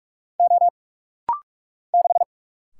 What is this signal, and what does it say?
Morse code, heard as a keyed tone, about 46 wpm, 700 Hz. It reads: O 6
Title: Single Letter-Number Alt - Rapid-Fire 45wpm